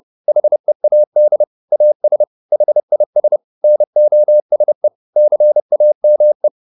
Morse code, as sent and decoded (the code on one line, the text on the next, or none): .... .- -.. / .- ... / .... .. ... / -. --- ... / -.-. .- -- .
HEAD AS HIS NOSE CAME